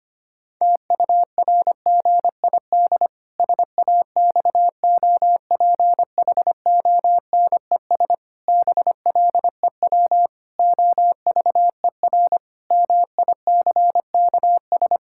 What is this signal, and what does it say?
Morse code, heard as a keyed tone, about 25 wpm, 700 Hz. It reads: TURGID SAXOPHONES BLEW OVER MICKS
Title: Pangram Challenge v2 - 25wpm